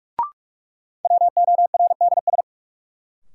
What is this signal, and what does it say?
Morse code, heard as a keyed tone, about 45 wpm, 700 Hz. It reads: WORDS
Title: Sentences from Top 1000 Words - Review Part 6 - New Word Drill - 45wpm